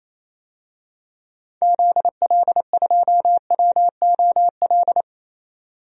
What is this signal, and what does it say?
Morse code, heard as a keyed tone, about 28 wpm, 700 Hz. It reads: ZL2WOL
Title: DXCC-100 Call Signs - 6 Characters - 28wpm